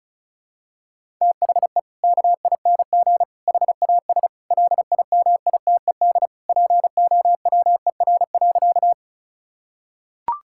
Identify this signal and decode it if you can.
Morse code, heard as a keyed tone, about 36 wpm, 700 Hz. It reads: THE KING HAS LIMITED POWER.
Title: Sentences from Top 1000 Words - Review Part 6 - 35wpm